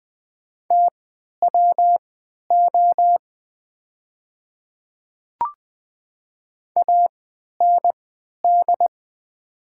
Morse code, heard as a keyed tone, about 20 wpm, spaced as for 11 wpm, 700 Hz. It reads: TWO AND
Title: 3x Character Spacing - Top 100 Words - 20wpm